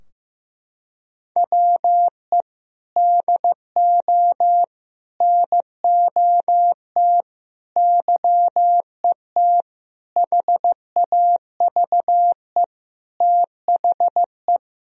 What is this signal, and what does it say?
Morse code, heard as a keyed tone, about 16 wpm, 700 Hz. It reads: WE DO NOT YET HAVE THE
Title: Sentences from Top 1000 Words - Review Part 3 - 15wpm